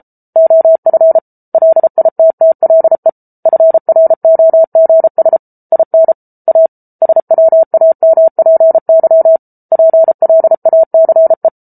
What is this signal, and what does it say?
Morse code, heard as a keyed tone, about 34 wpm, 650 Hz. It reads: OF LITTLE FROGS IN A SWAMPY PLACE